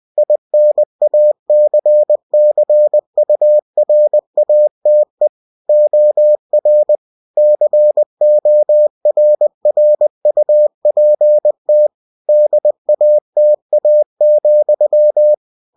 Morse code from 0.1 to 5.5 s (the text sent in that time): INACCURATE